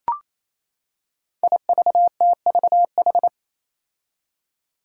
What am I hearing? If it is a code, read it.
Morse code, heard as a keyed tone, about 28 wpm, 700 Hz. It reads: IVTVH
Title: Five Challenging Letter-Number Combinations 28wpm